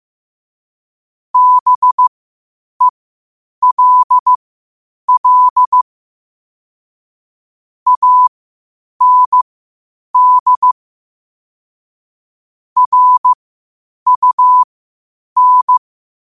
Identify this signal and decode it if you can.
Morse code, heard as a keyed tone, about 15 wpm, spaced as for 8 wpm, 1000 Hz. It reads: BELL AND RUN
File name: qotdfiles / 8wpm